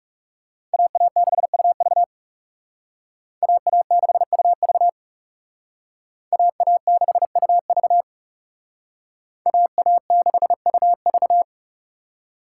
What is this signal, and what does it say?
Morse code, heard as a keyed tone, about 38 wpm, 700 Hz. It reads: AA6UV AA6UV AA6UV AA6UV